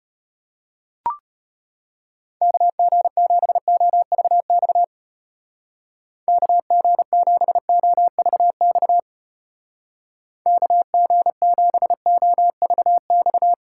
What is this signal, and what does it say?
Morse code, heard as a keyed tone, about 31 wpm, 700 Hz. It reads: KG7OVX KG7OVX KG7OVX
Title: Speed Racing - USA 2x3 Call Signs 25wpm